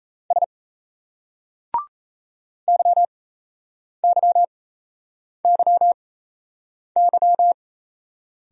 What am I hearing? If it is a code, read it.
Morse code, heard as a keyed tone, about 38 wpm, spaced as for 8 wpm, 700 Hz. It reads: S YYYY